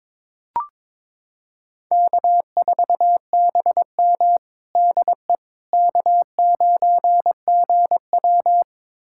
Morse code, heard as a keyed tone, about 22 wpm, 700 Hz. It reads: K4BM DE K9GW
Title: USA Callsign DE Callsign - 4 Characters - 22wpm